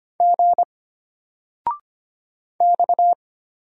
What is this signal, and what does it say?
Morse code, heard as a keyed tone, about 25 wpm, 700 Hz. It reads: G X
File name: Single-Letter-Number-25wpm